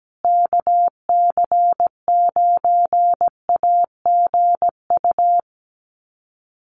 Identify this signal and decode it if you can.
Morse code, heard as a keyed tone, about 17 wpm, 700 Hz. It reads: KC9AGU
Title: USA Call Signs Repeated 2x 17wpm